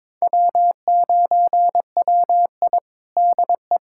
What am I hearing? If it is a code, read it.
Morse code, heard as a keyed tone, about 22 wpm, 700 Hz. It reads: W9WI DE
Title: USA Callsign DE Callsign - 4 Characters - 22wpm